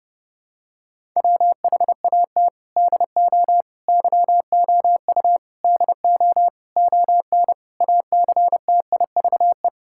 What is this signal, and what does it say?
Morse code, heard as a keyed tone, about 30 wpm, 700 Hz. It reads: WHAT DO YOU DO ON ACTIVE
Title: Sentences from Top 1000 Words - Review Part 6 - 30wpm